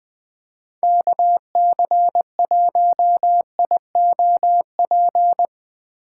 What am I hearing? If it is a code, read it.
Morse code, heard as a keyed tone, about 20 wpm, 700 Hz. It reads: KC1IOP
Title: USA Call Signs - 6 Characters - 4x Character Spacing - 20wpm